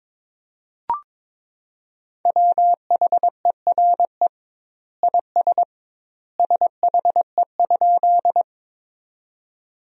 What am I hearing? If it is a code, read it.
Morse code, heard as a keyed tone, about 22 wpm, 700 Hz. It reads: WHERE IS SHE?
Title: 2x Word-Spacing - Sentences from Top 200 Words - 22wpm